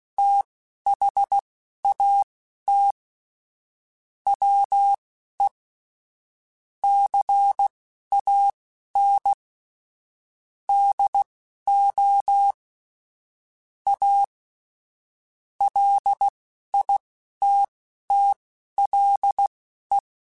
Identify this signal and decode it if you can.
Morse code, heard as a keyed tone, about 16 wpm, spaced as for 11 wpm, 800 Hz. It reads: THAT WE CAN DO A LITTLE
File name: QSOs / QSOs-mp3s / WD0BGP 2@10